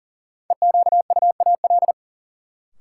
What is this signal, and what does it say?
Morse code, heard as a keyed tone, about 40 wpm, 700 Hz. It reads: EQUAL
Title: Sentences from Top 1400 Words - New Word Drill - 40wpm